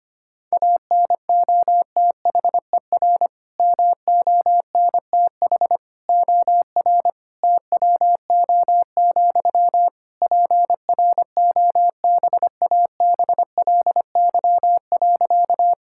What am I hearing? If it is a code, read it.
Morse code, heard as a keyed tone, about 25 wpm, 700 Hz. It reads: ANOTHER MONTH OR TWO, PROBABLY.
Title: Sentences from Top 1000 Words - Review Part 5 - 25wpm